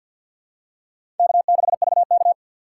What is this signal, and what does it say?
Morse code, heard as a keyed tone, about 53 wpm, 700 Hz. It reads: K6VK
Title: USA Call Signs - 4 Characters - 3x Character Spacing - 50wpm